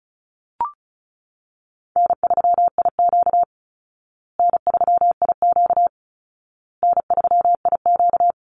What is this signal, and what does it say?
Morse code, heard as a keyed tone, about 34 wpm, 700 Hz. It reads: N3IQ N3IQ N3IQ
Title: Easy Call Signs Repeated 3x - 35wpm